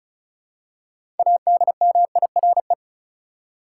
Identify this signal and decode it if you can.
Morse code, heard as a keyed tone, about 36 wpm, 700 Hz. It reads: ADMIRE
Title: Six Letter Words - 35wpm